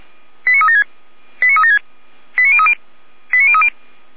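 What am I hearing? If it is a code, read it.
Five-tone selective call (ZVEI), heard as quick runs of five tones, tones of about 80 ms. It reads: 89376 89376 89030 89030
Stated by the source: Five-tone ZVEI selective call 89140 received via SDR, fire service Orlamünde.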